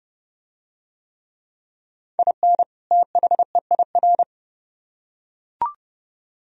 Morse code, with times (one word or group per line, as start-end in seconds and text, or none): 2.2-2.6 IN
2.9-4.2 THEIR